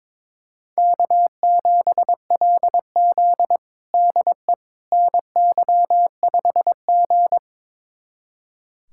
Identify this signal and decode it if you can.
Morse code, heard as a keyed tone, about 22 wpm, 700 Hz. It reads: K7LZ DE NY5G
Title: USA Callsign DE Callsign - 4 Characters - 22wpm